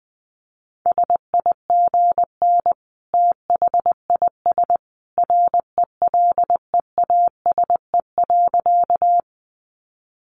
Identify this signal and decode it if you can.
Morse code, heard as a keyed tone, about 20 wpm, 700 Hz. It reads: SIGN THIS RELEASE.
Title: A Sound Of Thunder 20wpm